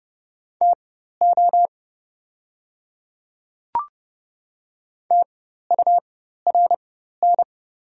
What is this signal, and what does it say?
Morse code, heard as a keyed tone, about 30 wpm, spaced as for 14 wpm, 700 Hz. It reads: TO TURN
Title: 4x Character Spacing - Top 200 Words - 30wpm